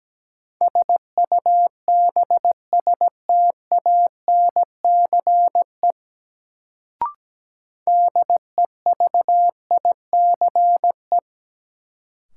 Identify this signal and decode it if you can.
Morse code, heard as a keyed tone, about 17 wpm, 700 Hz. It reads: SUBSTANCE DEVICE
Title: Sentences from Top 1500 Words - New Word Drill - 17wpm